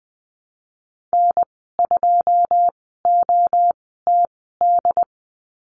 Morse code, heard as a keyed tone, about 19 wpm, spaced as for 14 wpm, 700 Hz. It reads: N2OTD
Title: 147 Lesson 24g - Letter y - Call Signs 20wpm